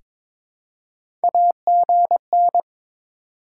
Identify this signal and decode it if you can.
Morse code, heard as a keyed tone, about 21 wpm, 700 Hz. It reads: AGN
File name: Top 90 QSO Elements - 3x Character Spacing - 22wpm